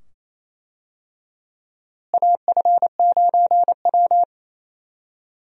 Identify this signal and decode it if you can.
Morse code, heard as a keyed tone, about 28 wpm, 700 Hz. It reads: AF9W
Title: USA Call Signs - 4 Characters - Repeated 2x - 28wpm